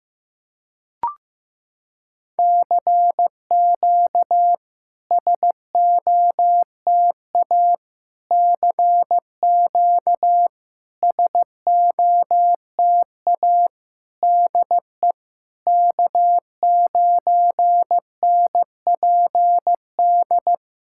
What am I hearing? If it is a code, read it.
Morse code, heard as a keyed tone, about 15 wpm, 700 Hz. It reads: CQ SOTA CQ SOTA DE K9NPD
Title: SOTA QSO Practice 15wpm